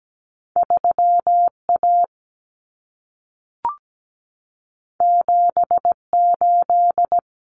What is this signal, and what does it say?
Morse code, heard as a keyed tone, about 17 wpm, 700 Hz. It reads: 3A 78
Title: Common Cut Numbers - 2-Digit 17wpm